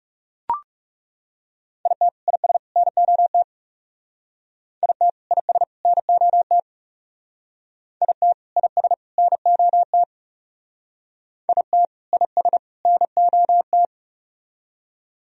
Morse code, heard as a keyed tone, about 41 wpm, 700 Hz. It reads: IT IS NOT IT IS NOT IT IS NOT IT IS NOT